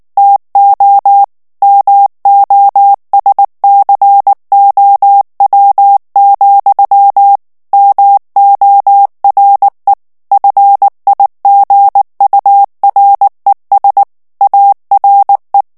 Morse code, heard as a keyed tone, about 19 wpm, 800 Hz. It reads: TO MOSCOW, MORE FIGURES ARE
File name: morse mp3 / ANGLAIS texte clair 2 vitesse 18 mots minute